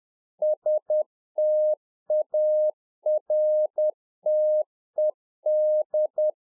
Morse code, heard as a keyed tone, about 10 wpm, 610 Hz. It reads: STARTED
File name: 2011-04-mp3 / A SMALL TRANCEIVER 1 10WPM